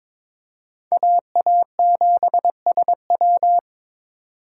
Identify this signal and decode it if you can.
Morse code, heard as a keyed tone, about 22 wpm, 700 Hz. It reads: AA7SW
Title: USA Call Signs - 4x Character Spacing - 22wpm